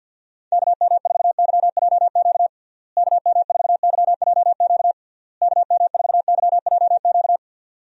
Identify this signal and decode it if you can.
Morse code, heard as a keyed tone, about 53 wpm, 700 Hz. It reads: KM4YJQ KM4YJQ KM4YJQ
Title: USA Call Signs - 6 Characters - Repeated 3x - 50wpm